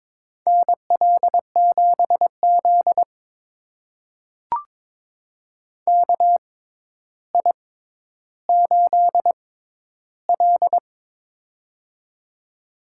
Morse code, heard as a keyed tone, about 22 wpm, spaced as for 11 wpm, 700 Hz. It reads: NL7Z KI8L
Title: USA Call Signs - 4 Characters - 6x Character Spacing - 22wpm